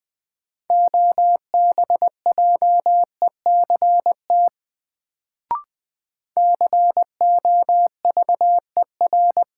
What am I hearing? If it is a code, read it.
Morse code, heard as a keyed tone, about 20 wpm, 700 Hz. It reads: OBJECT COVER
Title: Sentences from Top 500 Words - New Word Drill - 20wpm